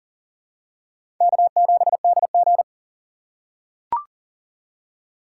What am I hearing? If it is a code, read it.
Morse code, heard as a keyed tone, about 40 wpm, 700 Hz. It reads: K7DG
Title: USA Call Signs - 4 Characters - 2x Character Spacing - 40wpm